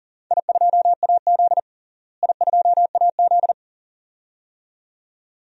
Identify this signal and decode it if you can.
Morse code, heard as a keyed tone, about 40 wpm, 700 Hz. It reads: I2AZ I2AZ
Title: DXCC-100 Call Signs - 4 Characters - Repeated 2x - 40wpm